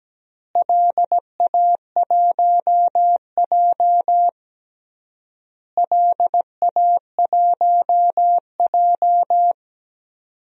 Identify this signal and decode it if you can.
Morse code, heard as a keyed tone, about 17 wpm, 700 Hz. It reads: LA1J LA1J
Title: DXCC-100 Call Signs - 4 Characters - Repeated 3x - 17wpm